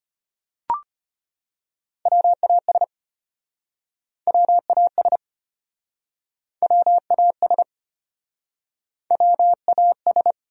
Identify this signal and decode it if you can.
Morse code, heard as a keyed tone, about 33 wpm, 700 Hz. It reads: WAS WAS WAS WAS